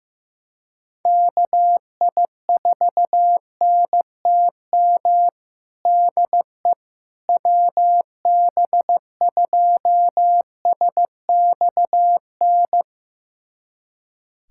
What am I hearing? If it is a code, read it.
Morse code, heard as a keyed tone, about 15 wpm, 700 Hz. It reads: KI4NTM DE WB2SXN
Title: USA Callsign DE Callsign 15wpm